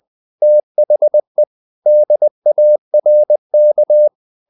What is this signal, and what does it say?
Morse code, heard as a keyed tone, about 20 wpm, 600 Hz. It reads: THE DARK
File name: the old man and the sea part 2 20wpm